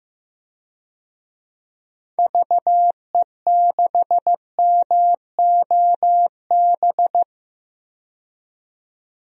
Morse code, heard as a keyed tone, about 15 wpm, 700 Hz. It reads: VE6MOB
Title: DXCC-100 Call Signs - 6 Characters - 15wpm